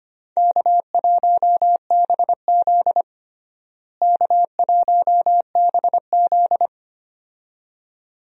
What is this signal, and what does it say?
Morse code, heard as a keyed tone, about 25 wpm, 700 Hz. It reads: K1BZ K1BZ
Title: USA Call Signs - 4 Characters - Repeated 2x - 25wpm